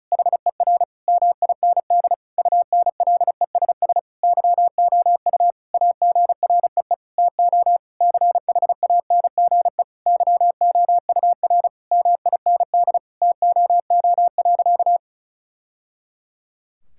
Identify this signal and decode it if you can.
Morse code, heard as a keyed tone, about 36 wpm, 700 Hz. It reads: HER MIND UNLESS YOU AGREE TO CHANGE YOUR MIND TOO.
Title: Sentences from Top 1000 Words - Review Part 6 - 35wpm